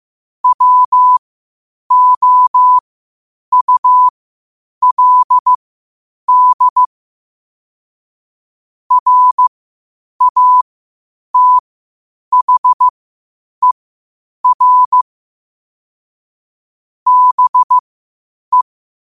Morse code, heard as a keyed tone, about 15 wpm, spaced as for 8 wpm, 1000 Hz. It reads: WOULD RATHER BE
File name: qotdfiles / 8wpm